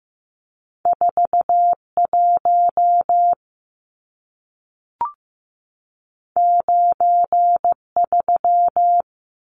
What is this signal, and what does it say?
Morse code, heard as a keyed tone, about 15 wpm, 700 Hz. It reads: 41 93
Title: Numbers - 2-Digit 15wpm